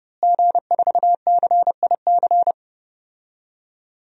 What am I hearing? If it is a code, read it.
Morse code, heard as a keyed tone, about 32 wpm, 700 Hz. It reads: G4CIC
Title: DXCC-100 Call Signs - 5 Characters - Repeated 3x - 30wpm